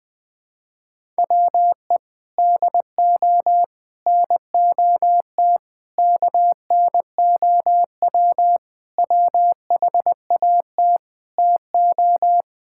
Morse code, heard as a keyed tone, about 20 wpm, 700 Hz. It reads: WE DO NOT KNOW WHAT TO
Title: Sentences with 5-Word Phrases - Spoken Beginning - 20wpm